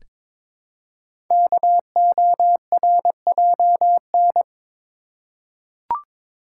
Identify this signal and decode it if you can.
Morse code, heard as a keyed tone, about 22 wpm, 700 Hz. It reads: KORJN
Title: Code Groups - 5 Letters - 22wpm